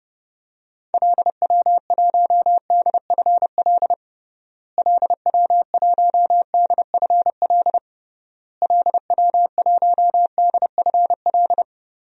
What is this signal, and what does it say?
Morse code, heard as a keyed tone, about 30 wpm, 700 Hz. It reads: LW1DFL LW1DFL LW1DFL
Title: DXCC-100 Call Signs - 6 Characters - Repeated 3x - 30wpm